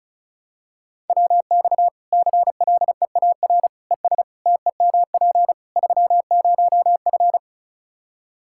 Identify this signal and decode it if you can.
Morse code, heard as a keyed tone, about 36 wpm, 700 Hz. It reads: WX CLEAR ES TEMP 30F
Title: Practice QSO - Exchange 2 - Info - 35wpm